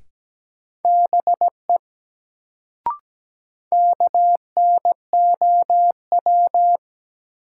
Morse code, heard as a keyed tone, about 17 wpm, 700 Hz. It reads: BE KNOW
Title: Sets of 2 Words - New Word Drill - 17wpm